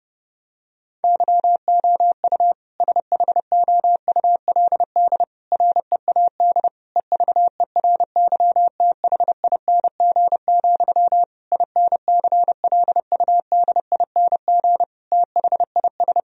Morse code, heard as a keyed tone, about 30 wpm, 700 Hz. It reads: YOU SHOULD READ EVERYTHING, INCLUDING THIS
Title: Sentences from Top 1000 Words - Review Part 4 - 30wpm